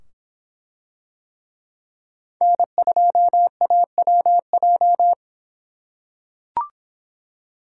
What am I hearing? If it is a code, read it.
Morse code, heard as a keyed tone, about 25 wpm, 700 Hz. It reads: N2AWJ